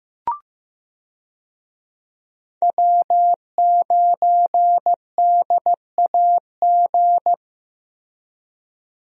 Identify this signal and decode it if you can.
Morse code, heard as a keyed tone, about 16 wpm, 700 Hz. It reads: W9DAG
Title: USA Call Signs - 5 Characters - Repeated 3x - 15wpm